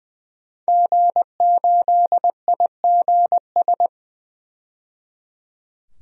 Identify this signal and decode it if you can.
Morse code, heard as a keyed tone, about 21 wpm, 700 Hz. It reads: G8IGS
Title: DXCC-100 Call Signs - 5 Characters - 20wpm